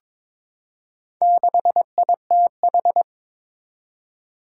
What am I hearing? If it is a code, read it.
Morse code, heard as a keyed tone, about 22 wpm, 700 Hz. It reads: 6ITH
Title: Four Challenging Letter-Number Combinations 22wpm